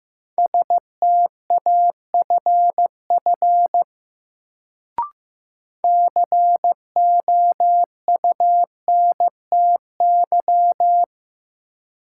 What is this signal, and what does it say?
Morse code, heard as a keyed tone, about 15 wpm, 700 Hz. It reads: STAFF COUNTY
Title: Sentences from Top 1200 Words - New Word Drill Alt - 15wpm